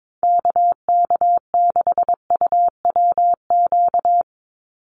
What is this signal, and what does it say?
Morse code, heard as a keyed tone, about 22 wpm, 700 Hz. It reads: KK6UWQ
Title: USA Call Signs 22wpm